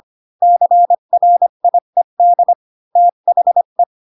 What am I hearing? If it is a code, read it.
Morse code, heard as a keyed tone, about 25 wpm, 700 Hz. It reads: CRIED THE
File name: Pinocchio / Pinocchio ch1 20 25 ik1hge